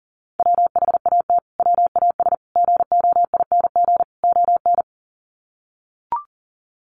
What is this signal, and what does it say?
Morse code, heard as a keyed tone, about 40 wpm, 700 Hz. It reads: WHAT WAS GOING ON
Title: Sets of 4 Words 40wpm